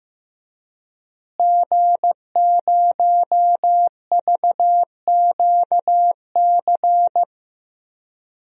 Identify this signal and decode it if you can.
Morse code, heard as a keyed tone, about 15 wpm, 700 Hz. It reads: G0VQC
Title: DXCC-100 Call Signs - 5 Characters - Repeated 2x - 15wpm